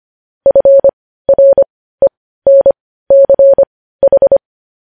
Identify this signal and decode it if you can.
Morse code, heard as a keyed tone, about 25 wpm, spaced as for 15 wpm, 550 Hz. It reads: FRENCH
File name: podcast / News headlines-1x-25x15